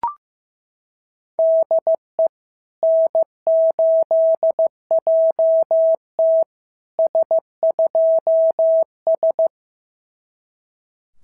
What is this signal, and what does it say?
Morse code, heard as a keyed tone, about 15 wpm, 650 Hz. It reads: DE N8JT S2S
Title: SOTA QSO Practice 15wpm